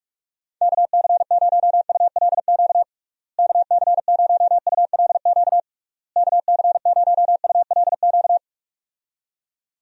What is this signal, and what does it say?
Morse code, heard as a keyed tone, about 42 wpm, 700 Hz. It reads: KC0ULQ KC0ULQ KC0ULQ
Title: USA Call Signs - 6 Characters - Repeated 3x - 45wpm